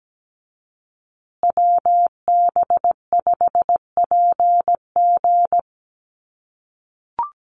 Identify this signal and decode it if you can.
Morse code, heard as a keyed tone, about 17 wpm, 700 Hz. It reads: WB5PG
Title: USA Call Signs Repeated 2x 17wpm